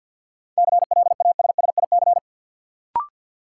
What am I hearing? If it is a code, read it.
Morse code, heard as a keyed tone, about 46 wpm, 700 Hz. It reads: CLASSIC
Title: Sentences from Top 1000 Words - New Word Drill - 50wpm